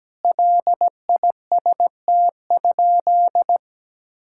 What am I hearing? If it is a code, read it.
Morse code, heard as a keyed tone, about 17 wpm, 700 Hz. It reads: LIST?
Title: Sentences from Top 1000 Words - Review Part 6 - 17wpm